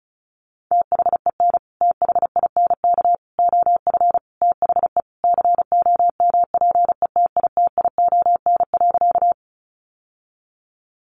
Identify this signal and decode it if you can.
Morse code, heard as a keyed tone, about 34 wpm, 700 Hz. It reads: THEN THINK OF THE COMPETITION.
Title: Sentences from Top 1000 Words 35wpm